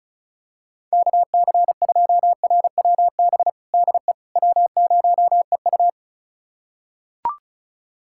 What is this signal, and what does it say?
Morse code, heard as a keyed tone, about 34 wpm, 700 Hz. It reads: KC2RWB DE W0EU
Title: USA Callsign DE Callsign 35wpm